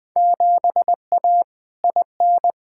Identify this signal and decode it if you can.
Morse code, heard as a keyed tone, about 20 wpm, 700 Hz. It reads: 7A IN
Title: Field Day Exchanges Encore - 20wpm